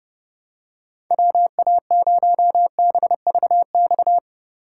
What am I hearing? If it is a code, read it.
Morse code, heard as a keyed tone, about 29 wpm, 700 Hz. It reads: WA0BVX